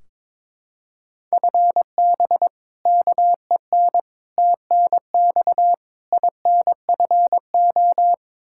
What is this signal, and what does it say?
Morse code, heard as a keyed tone, about 22 wpm, 700 Hz. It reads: FB KEN TNX INFO
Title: Practice QSO - Exchange 3 - More Info - 22wpm